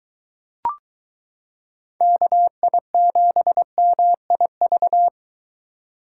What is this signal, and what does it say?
Morse code, heard as a keyed tone, about 22 wpm, 700 Hz. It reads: KI7MIV